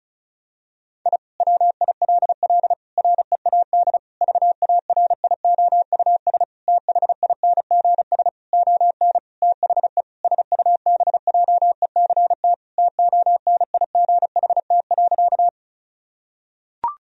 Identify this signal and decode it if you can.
Morse code, heard as a keyed tone, about 36 wpm, 700 Hz. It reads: I WILL READ VARIOUS THINGS ON THE SUBJECT TONIGHT.
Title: Sentences from Top 1000 Words - Review Part 4 - 35wpm